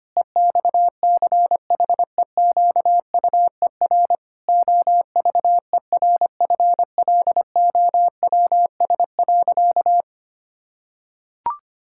Morse code, heard as a keyed tone, about 25 wpm, 700 Hz. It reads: EXCHEQUER OVERFLOWS.
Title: Pangram Challenge v2 - 25wpm